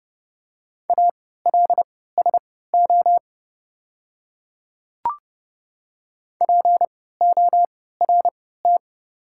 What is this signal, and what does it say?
Morse code, heard as a keyed tone, about 31 wpm, spaced as for 17 wpm, 690 Hz. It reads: ALSO PORT
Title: 3x Character Spacing - Top 200 Words - 30wpm